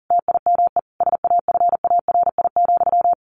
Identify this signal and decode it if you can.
Morse code, heard as a keyed tone, about 40 wpm, 700 Hz. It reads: TIME SAFARI,
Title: A Sound Of Thunder 40wpm